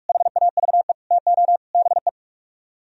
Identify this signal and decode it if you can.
Morse code, heard as a keyed tone, about 57 wpm, 700 Hz. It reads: HAVE TO BE